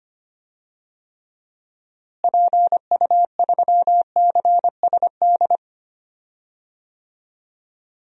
Morse code, heard as a keyed tone, about 25 wpm, 700 Hz. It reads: PU3CSD
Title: DXCC-100 Call Signs - 6 Characters - Repeated 2x - 25wpm